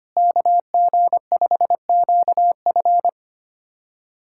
Morse code, heard as a keyed tone, about 25 wpm, 700 Hz. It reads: KG5QF
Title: USA Callsign DE Callsign 25wpm